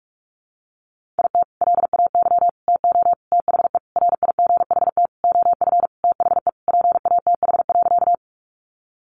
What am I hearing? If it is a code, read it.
Morse code, heard as a keyed tone, about 44 wpm, 700 Hz. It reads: IT LAY TO THE RIGHT OF THE PATH.
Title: Sentences from Top 1400 Words 45wpm